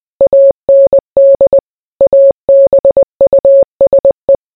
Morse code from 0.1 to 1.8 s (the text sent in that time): AND